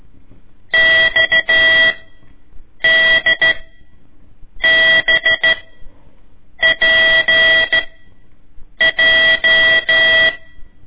Morse code, heard as a keyed tone, about 14 wpm, spaced as for 6 wpm, 1950 Hz. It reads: XDBPJ
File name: ik1weg2 / CORSI CW / lez4-1